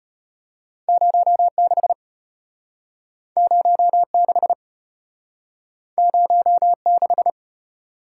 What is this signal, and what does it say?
Morse code, heard as a keyed tone, about 34 wpm, 700 Hz. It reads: 06 06 06